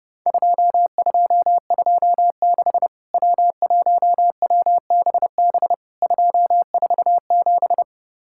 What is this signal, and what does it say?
Morse code, heard as a keyed tone, about 30 wpm, 700 Hz. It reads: 2226 W1WBB 247
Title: Worked All Europe DX Contest - QTC Exchange - 30wpm